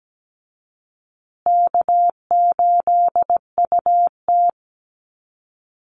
Morse code, heard as a keyed tone, about 17 wpm, 700 Hz. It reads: K8UT
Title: Call Signs Repeated 3x - 17wpm